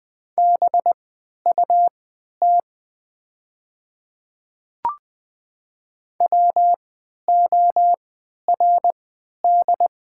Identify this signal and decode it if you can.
Morse code, heard as a keyed tone, about 20 wpm, spaced as for 11 wpm, 700 Hz. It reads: BUT WORD
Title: 3x Character Spacing - Top 100 Words - 20wpm